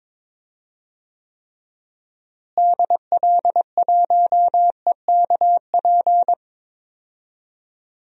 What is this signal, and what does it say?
Morse code, heard as a keyed tone, about 21 wpm, 700 Hz. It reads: DL1EKP
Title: DXCC-100 Call Signs - 6 Characters - Repeated 2x - 22wpm